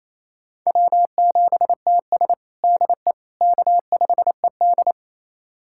Morse code, heard as a keyed tone, about 28 wpm, 700 Hz. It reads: W7TS DE K5ED
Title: USA Callsign DE Callsign - 4 Characters - 28wpm